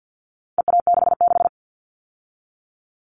Morse code, heard as a keyed tone, about 50 wpm, 700 Hz. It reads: EA66
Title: All Cut Numbers - 4-Digit 50wpm